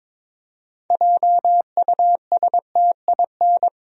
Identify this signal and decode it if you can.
Morse code, heard as a keyed tone, about 22 wpm, 700 Hz. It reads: JUSTIN
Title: USA Common Names - 22wpm